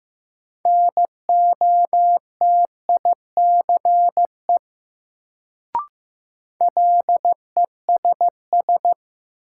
Morse code, heard as a keyed tone, about 15 wpm, 700 Hz. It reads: NOTICE LESS
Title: Sentences from Top 400 Words - New Word Drill - 15wpm